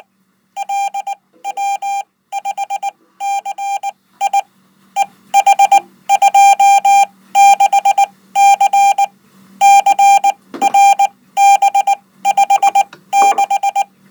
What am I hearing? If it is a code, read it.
Morse code, heard as a keyed tone, about 19 wpm, spaced as for 15 wpm, 800 Hz. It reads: LW5CI EH26C CRB56
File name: Code